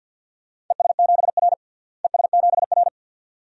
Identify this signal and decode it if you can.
Morse code, heard as a keyed tone, about 47 wpm, 700 Hz. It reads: ES7R ES7R
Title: DXCC-100 Call Signs - 4 Characters - Repeated 2x - 50wpm